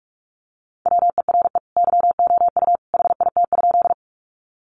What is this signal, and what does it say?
Morse code, heard as a keyed tone, about 44 wpm, 700 Hz. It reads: WERE YOU HIT?
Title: Sentences from Top 1200 Words 45wpm